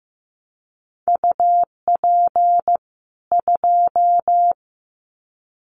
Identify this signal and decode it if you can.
Morse code, heard as a keyed tone, about 15 wpm, 700 Hz. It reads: UP 2
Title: Common QSO Phrases 15wpm